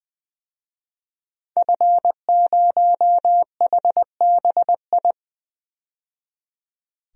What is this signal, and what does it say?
Morse code, heard as a keyed tone, about 20 wpm, 700 Hz. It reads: F0HBI
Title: DXCC-100 Call Signs - 5 Characters - 20wpm